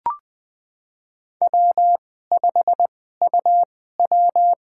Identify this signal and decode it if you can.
Morse code, heard as a keyed tone, about 20 wpm, spaced as for 14 wpm, 700 Hz. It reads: W5UW
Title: USA Call Signs - 2x Character Spacing - 20wpm